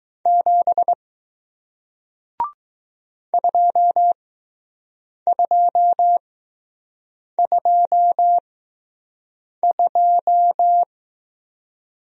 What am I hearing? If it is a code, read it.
Morse code, heard as a keyed tone, about 20 wpm, spaced as for 6 wpm, 700 Hz. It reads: 7 2222